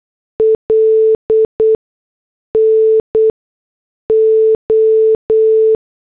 Morse code, heard as a keyed tone, about 8 wpm, 440 Hz. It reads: L N O
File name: morse / headlines